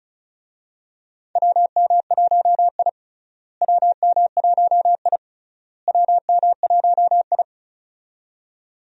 Morse code, heard as a keyed tone, about 36 wpm, 700 Hz. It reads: WM1I WM1I WM1I
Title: USA Call Signs - 4 Characters - Repeated 3x - 35wpm